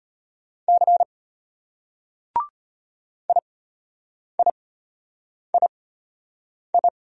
Morse code, heard as a keyed tone, about 34 wpm, spaced as for 8 wpm, 700 Hz. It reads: C IIII